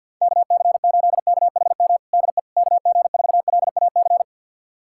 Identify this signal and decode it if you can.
Morse code, heard as a keyed tone, about 48 wpm, 700 Hz. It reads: KC8KHM DE KG4LAC
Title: USA Callsign DE Callsign - 6 Characters - 50wpm